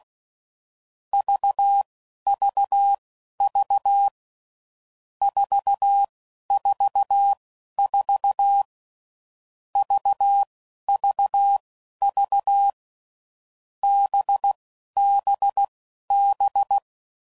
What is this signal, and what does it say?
Morse code, heard as a keyed tone, about 16 wpm, spaced as for 11 wpm, 800 Hz. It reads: VVV 444 VVV BBB
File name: Groups / Basic Letter Groups / VB46